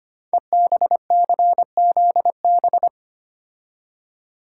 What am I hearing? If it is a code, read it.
Morse code, heard as a keyed tone, about 25 wpm, 700 Hz. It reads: EBCZB
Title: Code Groups - 5 Letters - 25wpm